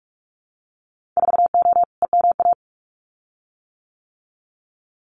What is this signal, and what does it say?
Morse code, heard as a keyed tone, about 44 wpm, 700 Hz. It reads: 4O EMA
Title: Winter Field Day Exchanges 45wpm